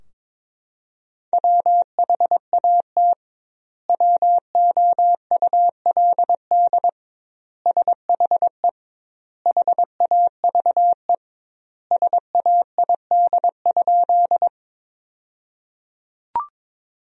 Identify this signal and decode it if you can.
Morse code, heard as a keyed tone, about 21 wpm, 700 Hz. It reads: WHAT WOULD SHE HAVE SAID?
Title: 2x Word-Spacing - Sentences from Top 100 Words - 22wpm